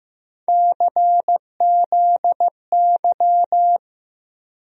Morse code, heard as a keyed tone, about 15 wpm, 700 Hz. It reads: CZY
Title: USA Call Sign Postfixes - 3 Letters 15wpm